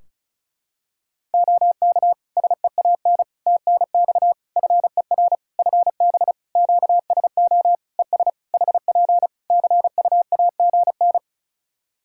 Morse code, heard as a keyed tone, about 36 wpm, 700 Hz. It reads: OK SEAN TNX FER FB QSO ES HP CUAGN
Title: Practice QSO - Exchange 4 - Ending - 35wpm